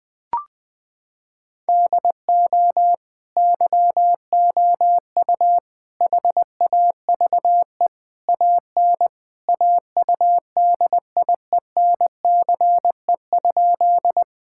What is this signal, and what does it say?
Morse code, heard as a keyed tone, about 20 wpm, 700 Hz. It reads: DO YOU HAVE AN AUDIENCE?
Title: Sentences from Top 1000 Words - Review Part 5 - 20wpm